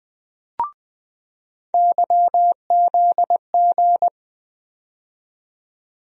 Morse code, heard as a keyed tone, about 20 wpm, 700 Hz. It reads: YZG
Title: USA Call Sign Postfixes - 3 Letters 20wpm